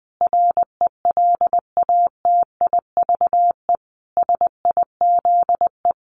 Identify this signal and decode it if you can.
Morse code, heard as a keyed tone, about 20 wpm, 700 Hz. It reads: RELATIVE SIZE
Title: Sentences from Top 1400 Words 20wpm